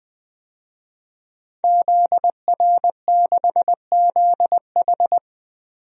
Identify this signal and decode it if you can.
Morse code, heard as a keyed tone, about 21 wpm, 700 Hz. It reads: ZR6ZH
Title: DXCC-100 Call Signs - 5 Characters - Repeated 3x - 20wpm